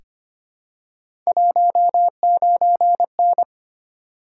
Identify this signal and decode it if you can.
Morse code, heard as a keyed tone, about 25 wpm, 700 Hz. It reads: 19N
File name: Three Letter-Number Combinations - 25wpm